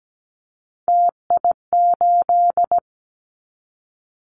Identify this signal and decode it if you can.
Morse code, heard as a keyed tone, about 17 wpm, 700 Hz. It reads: TI8
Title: Callsign Prefixes for Top 100 DXCC - 17wpm